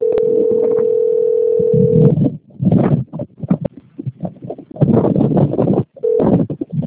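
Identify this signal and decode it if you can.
Ringback tone (North America), 440 + 480 Hz, with noise.
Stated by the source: Outgoing telephone call recording from the Cosmopolis - SP system.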